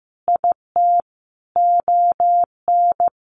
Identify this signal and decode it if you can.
Morse code, heard as a keyed tone, about 15 wpm, 700 Hz. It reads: IT ON
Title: Sentences from Top 1400 Words 15wpm